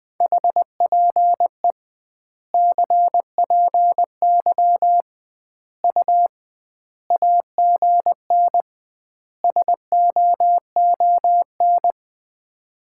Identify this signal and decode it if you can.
Morse code, heard as a keyed tone, about 20 wpm, 700 Hz. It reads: HPE CPY U AGN SOON
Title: Common QSO Phrases - 2x Word-Spacing - 20wpm